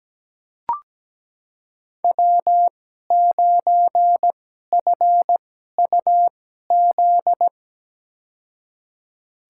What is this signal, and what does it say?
Morse code, heard as a keyed tone, about 17 wpm, spaced as for 12 wpm, 700 Hz. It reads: W9FUZ
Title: USA Call Signs - 5 Characters - 2x Character Spacing - 17wpm